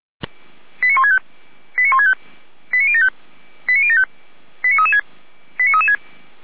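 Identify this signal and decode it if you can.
Five-tone selective call (ZVEI), heard as quick runs of five tones, tones of about 70 ms. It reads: 89165 89165 89075 89075 89336 89336
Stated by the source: ZVEI five-tone selective call received by SDR, from the feuerwehr-orlamuende.org fire brigade archive.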